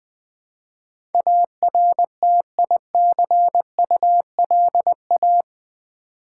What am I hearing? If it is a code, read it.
Morse code, heard as a keyed tone, about 20 wpm, 700 Hz. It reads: ARTICULA
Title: Build A Word - 4 Letters At A Time - 10-Letter Words - 20wpm